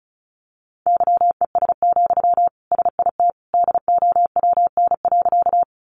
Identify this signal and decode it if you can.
Morse code, heard as a keyed tone, about 34 wpm, 700 Hz. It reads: YES, SIT DOWN.
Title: Sentences from Top 500 Words 35wpm